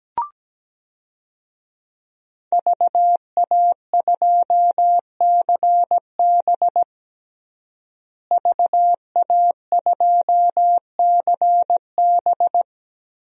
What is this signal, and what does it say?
Morse code, heard as a keyed tone, about 17 wpm, 700 Hz. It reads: VA2CB VA2CB
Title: DXCC-100 Call Signs - 5 Characters - Repeated 2x - 17wpm